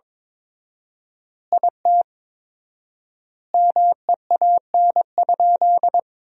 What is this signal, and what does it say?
Morse code, heard as a keyed tone, about 22 wpm, 700 Hz. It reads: IT MEAN?
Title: 4x Word-Spacing - Sentences from Top 200 Words - 22wpm